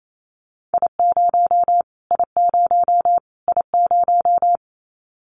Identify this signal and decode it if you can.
Morse code, heard as a keyed tone, about 28 wpm, 700 Hz. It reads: I0 I0 I0
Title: Callsign Prefixes for Top 25 DXCC - Repeated 3x - 28wpm